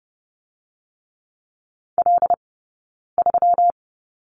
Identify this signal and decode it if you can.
Morse code, heard as a keyed tone, about 30 wpm, 700 Hz. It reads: L 3
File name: Lesson 20d - Number 3 - Two Characters 30wpm